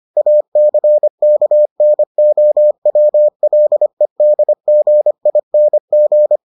Morse code, heard as a keyed tone, about 25 wpm, 600 Hz. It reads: ACKNOWLEDGING